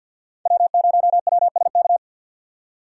Morse code, heard as a keyed tone, about 52 wpm, 700 Hz. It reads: W0WSK
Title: USA Call Signs - 6x Character Spacing - 50wpm